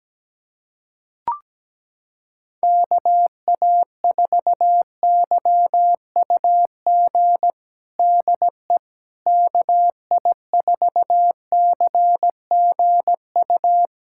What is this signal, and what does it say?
Morse code, heard as a keyed tone, about 17 wpm, 700 Hz. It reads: KA4YUG DE KI4CGU
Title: USA Callsign DE Callsign - 6 Characters - 17wpm